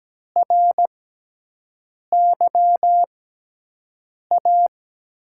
Morse code, heard as a keyed tone, about 17 wpm, spaced as for 6 wpm, 700 Hz. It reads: RYA
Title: USA Call Signs - 5 Characters - 6x Character Spacing - 17wpm